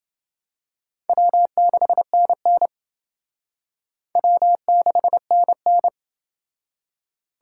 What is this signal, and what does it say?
Morse code, heard as a keyed tone, about 28 wpm, 700 Hz. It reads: W6NN W6NN